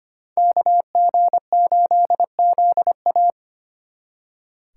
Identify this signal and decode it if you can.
Morse code, heard as a keyed tone, about 32 wpm, 700 Hz. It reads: KG8ZA